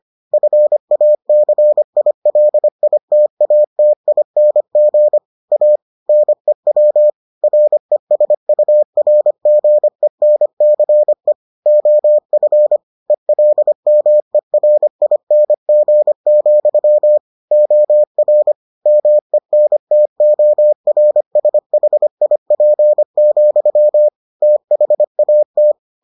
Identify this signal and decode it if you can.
Morse code, heard as a keyed tone, about 25 wpm, 600 Hz. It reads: FACILITATING A NEW RESURGENCE OF ELMERING, OR MENTORSHIP, THAT